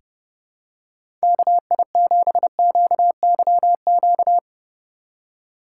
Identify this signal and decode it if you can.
Morse code, heard as a keyed tone, about 30 wpm, 700 Hz. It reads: KI7QYQ
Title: USA Call Signs - 6 Characters - 4x Character Spacing - 30wpm